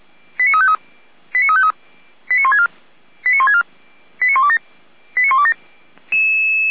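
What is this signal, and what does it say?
Five-tone selective call (ZVEI), heard as quick runs of five tones, tones of about 80 ms. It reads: 89353 89353 89164 89164 89127 89127
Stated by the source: ZVEI five-tone selective call 89502 received by SDR and logged by Feuerwehr Orlamünde, a fire brigade.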